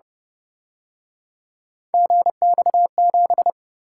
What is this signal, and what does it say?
Morse code, heard as a keyed tone, about 31 wpm, 700 Hz. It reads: GX7
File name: Speed-Racing - Three Letter-Number Combinations - 25wpm